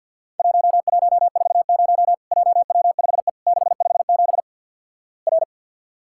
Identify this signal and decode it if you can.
Morse code, heard as a keyed tone, about 50 wpm, 700 Hz. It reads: 1140 JW5E 657 R
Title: Worked All Europe DX Contest - QTC Exchange with Repeats - 50wpm